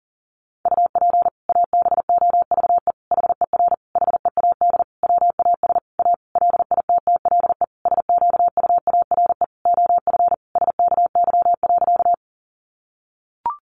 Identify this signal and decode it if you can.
Morse code, heard as a keyed tone, about 37 wpm, 700 Hz. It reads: UP ABOVE HER HEAD WAS A LITTLE SQUARE OF SKY.
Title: Sentences from Top 1200 Words 40wpm